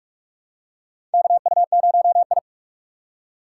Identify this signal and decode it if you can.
Morse code, heard as a keyed tone, about 42 wpm, 700 Hz. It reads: KU0I
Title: USA Call Signs - 4 Characters - 3x Character Spacing - 45wpm